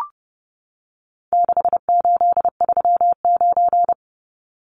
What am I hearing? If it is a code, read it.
Morse code, heard as a keyed tone, about 30 wpm, 700 Hz. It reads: 6839
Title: Numbers - 4-Digit 30wpm